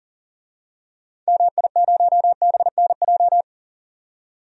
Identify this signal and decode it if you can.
Morse code, heard as a keyed tone, about 39 wpm, 700 Hz. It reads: MI0BNJ
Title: DXCC-100 Call Signs - 6 Characters - 40wpm